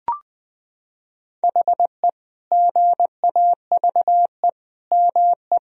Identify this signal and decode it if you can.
Morse code, heard as a keyed tone, about 20 wpm, 700 Hz. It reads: HE GAVE ME
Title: Sentences from Top 1000 Words - Review Part 4 - 20wpm